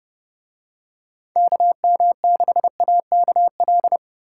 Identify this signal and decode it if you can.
Morse code, heard as a keyed tone, about 28 wpm, 700 Hz. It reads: KM6AKL